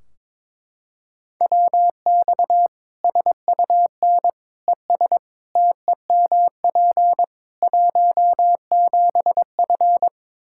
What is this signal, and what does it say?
Morse code, heard as a keyed tone, about 22 wpm, 700 Hz. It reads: WX SUN ES TEMP 17F
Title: K1USN SST QSO Practice 22wpm